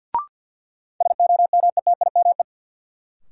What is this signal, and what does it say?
Morse code, heard as a keyed tone, about 49 wpm, 700 Hz. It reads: SOMETIME
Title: Sentences from Top 1000 Words - Review Part 6 - New Word Drill - 50wpm